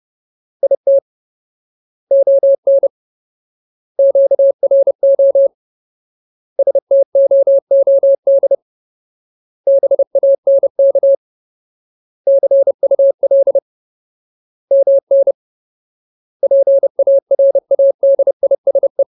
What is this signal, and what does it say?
Morse code, heard as a keyed tone, about 30 wpm, 550 Hz. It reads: IT ON QRO STOOD BANK CUL MN PARADISE
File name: Ultimate Word List / 30WPM Ultimate